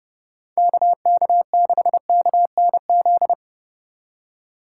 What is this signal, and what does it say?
Morse code, heard as a keyed tone, about 30 wpm, 700 Hz. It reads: KK6KNZ
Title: USA Call Signs - 6 Characters - 3x Character Spacing - 30wpm